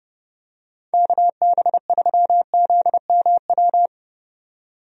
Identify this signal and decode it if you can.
Morse code, heard as a keyed tone, about 30 wpm, 700 Hz. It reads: KB3ZMW
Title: USA Call Signs - 6 Characters - 30wpm